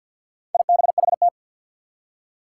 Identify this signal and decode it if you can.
Morse code, heard as a keyed tone, about 51 wpm, 700 Hz. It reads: IBHT
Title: Four Challenging Letter-Number Combinations 50wpm